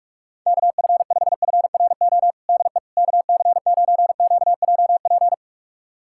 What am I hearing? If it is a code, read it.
Morse code, heard as a keyed tone, about 43 wpm, 700 Hz. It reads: KF5FRO DE KC9QJP
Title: USA Callsign DE Callsign - 6 Characters - 45wpm